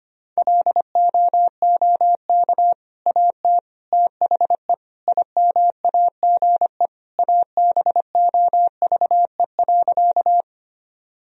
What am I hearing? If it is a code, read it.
Morse code, heard as a keyed tone, about 25 wpm, 690 Hz. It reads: LOOK AT THE IMAGE ABOVE.
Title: Sentences from Top 1000 Words - Review Part 4 - 25wpm